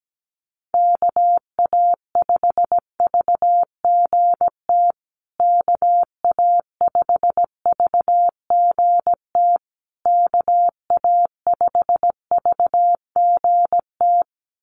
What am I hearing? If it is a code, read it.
Morse code, heard as a keyed tone, about 17 wpm, 700 Hz. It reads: KA5VGT KA5VGT KA5VGT
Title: USA Call Signs Repeated 3x 17wpm